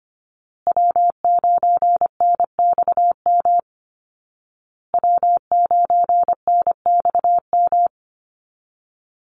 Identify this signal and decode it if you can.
Morse code, heard as a keyed tone, about 25 wpm, 700 Hz. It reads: W9NXM W9NXM
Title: Call Signs Repeated 2x - 25wpm